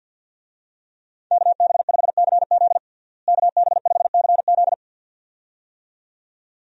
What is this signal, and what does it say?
Morse code, heard as a keyed tone, about 52 wpm, 700 Hz. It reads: KB5CZ KB5CZ
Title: USA Call Signs - 5 Characters - Repeated 2x - 50wpm